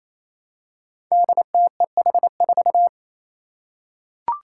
Morse code, heard as a keyed tone, about 27 wpm, 700 Hz. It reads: DTEH4
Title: Five Challenging Letter-Number Combinations 28wpm